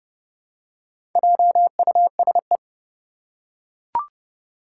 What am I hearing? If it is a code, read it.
Morse code, heard as a keyed tone, about 30 wpm, 700 Hz. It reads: JUSE
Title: Four Challenging Letter-Number Combinations 30wpm